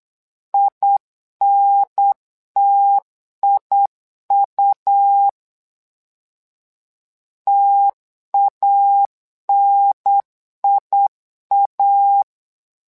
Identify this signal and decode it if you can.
Morse code, heard as a keyed tone, about 9 wpm, 800 Hz. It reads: INTIU TANIA
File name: LEZIONI / 2 UHDN / 2 UHDN 8wpm2232